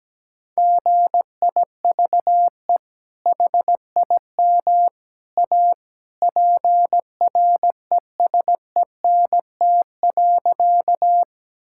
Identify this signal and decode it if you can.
Morse code, heard as a keyed tone, about 17 wpm, 700 Hz. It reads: GIVE HIM A PRESENT.
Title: Sentences from Top 1000 Words - Review Part 5 - 17wpm